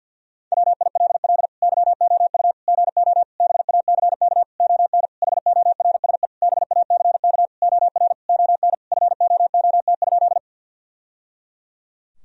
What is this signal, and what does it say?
Morse code, heard as a keyed tone, about 47 wpm, 700 Hz. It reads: WILL YOU GO BACK ON HORSE BACK OR ON FOOT?
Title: Sentences from Top 1000 Words - Review Part 3 - 50wpm